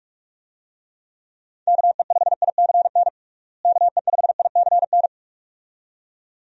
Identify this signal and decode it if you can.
Morse code, heard as a keyed tone, about 43 wpm, 700 Hz. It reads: KE5ICN KE5ICN
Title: USA Call Signs - 6 Characters - Repeated 2x - 45wpm